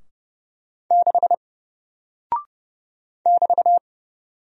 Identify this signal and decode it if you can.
Morse code, heard as a keyed tone, about 30 wpm, 700 Hz. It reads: 6 <BT>
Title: Single Character (Letter:Number:Punctuation:Prosign) Alt 30wpm